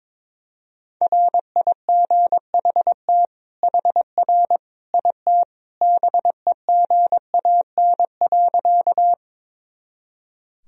Morse code, heard as a keyed tone, about 21 wpm, 700 Hz. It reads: RIGHT HR IT BEGAN.
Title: Sentences from Top 300 Words and Abbreviations 22wpm